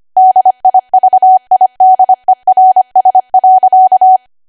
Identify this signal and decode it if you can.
Morse code, heard as a keyed tone, about 25 wpm, 750 Hz. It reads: DIVIDERS.
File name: Archive / 25 WPM / 25WPM